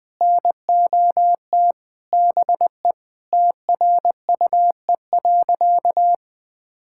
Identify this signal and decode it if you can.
Morse code, heard as a keyed tone, about 21 wpm, 700 Hz. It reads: NOT BE TRUE.
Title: Sentences with 5-Word Phrases - Spoken Beginning - 20wpm